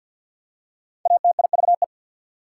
Morse code, heard as a keyed tone, about 48 wpm, 700 Hz. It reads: ATIVE
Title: Prefixes and Suffixes 50wpm